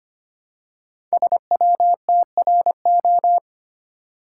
Morse code, heard as a keyed tone, about 25 wpm, 700 Hz. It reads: SWTRO
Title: Code Groups - 5 Letters - 25wpm